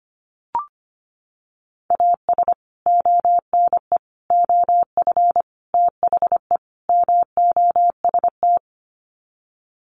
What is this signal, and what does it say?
Morse code, heard as a keyed tone, about 25 wpm, 700 Hz. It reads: AS ONE OF THE MOST
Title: Sets of 5 Words 25wpm